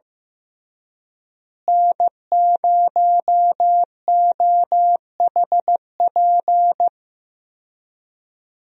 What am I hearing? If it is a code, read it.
Morse code, heard as a keyed tone, about 15 wpm, 700 Hz. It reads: N0OHP
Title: USA Call Signs - 5 Characters - Repeated 3x - 15wpm